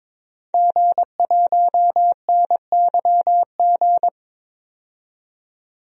Morse code, heard as a keyed tone, about 22 wpm, 700 Hz. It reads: G1NYG
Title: DXCC-100 Call Signs - 5 Characters - 22wpm